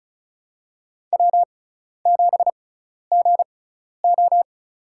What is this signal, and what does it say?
Morse code, heard as a keyed tone, about 36 wpm, spaced as for 12 wpm, 700 Hz. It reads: W7GO
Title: USA Call Signs - 6x Character Spacing - 35wpm